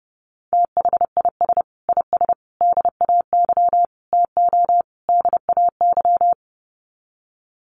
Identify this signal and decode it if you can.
Morse code, heard as a keyed tone, about 30 wpm, 700 Hz. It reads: THIS IS DAY TO DAY
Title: Sets of 5 Words Encore 30wpm